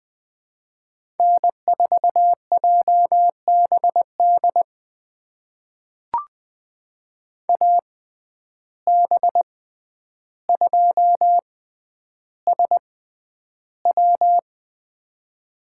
Morse code, heard as a keyed tone, about 20 wpm, spaced as for 10 wpm, 700 Hz. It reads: N4JBD AB2SW
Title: USA Call Signs - 5 Characters - 6x Character Spacing - 20wpm